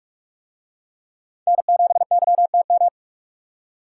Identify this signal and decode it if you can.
Morse code, heard as a keyed tone, about 44 wpm, 700 Hz. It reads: N7YTM